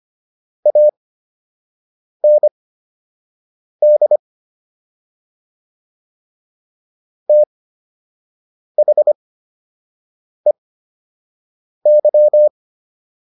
Morse code, heard as a keyed tone, about 25 wpm, spaced as for 6 wpm, 600 Hz. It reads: AND THEY